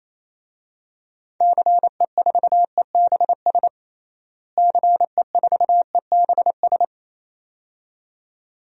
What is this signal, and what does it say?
Morse code, heard as a keyed tone, about 28 wpm, 700 Hz. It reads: CE4EBS CE4EBS
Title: DXCC-100 Call Signs - 6 Characters - Repeated 2x - 28wpm